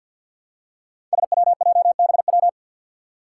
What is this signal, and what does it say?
Morse code, heard as a keyed tone, about 53 wpm, 700 Hz. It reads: SWJBW
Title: Five Challenging Letter-Number Combinations 50wpm